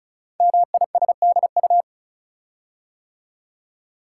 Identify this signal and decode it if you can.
Morse code, heard as a keyed tone, about 36 wpm, 690 Hz. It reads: MISDU
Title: Code Groups - 5 Letters - 35wpm